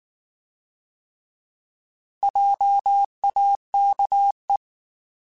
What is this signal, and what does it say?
Morse code, heard as a keyed tone, about 19 wpm, 800 Hz. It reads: JAKE
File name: Alphabetical / Alphabetical mp3s / Js @18